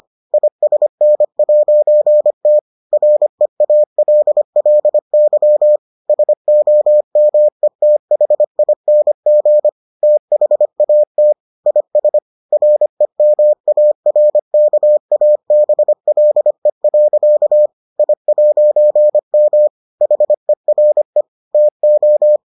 Morse code, heard as a keyed tone, about 25 wpm, 600 Hz. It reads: ISN'T REALLY SOMETHING THAT IS REMARKABLE. I'M HERE TO